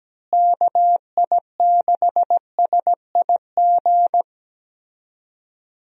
Morse code, heard as a keyed tone, about 17 wpm, 700 Hz. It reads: KI6SIG
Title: USA Call Signs - 6 Characters - Repeated 3x - 17wpm